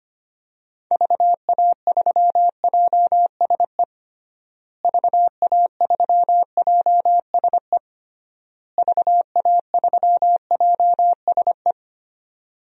Morse code, heard as a keyed tone, about 25 wpm, 700 Hz. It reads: VA3JSE VA3JSE VA3JSE
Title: DXCC-100 Call Signs - 6 Characters - Repeated 3x - 25wpm